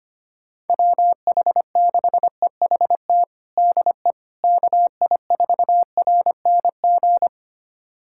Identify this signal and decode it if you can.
Morse code, heard as a keyed tone, about 25 wpm, 700 Hz. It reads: WH6EHT DE KI4RNG
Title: USA Callsign DE Callsign 25wpm